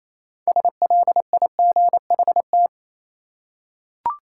Morse code, heard as a keyed tone, about 28 wpm, 700 Hz. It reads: SLIGHT
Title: Six Letter Words - 28wpm